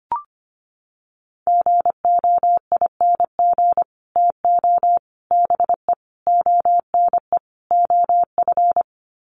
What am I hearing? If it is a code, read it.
Morse code, heard as a keyed tone, about 25 wpm, 700 Hz. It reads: GOING TO BE ONE OF
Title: Sets of 5 Words Encore2 25wpm